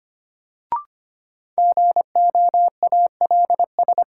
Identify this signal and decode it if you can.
Morse code, heard as a keyed tone, about 25 wpm, 700 Hz. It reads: GOALS
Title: Sentences from Top 1000 Words - Review Part 6 - New Word Drill - 25wpm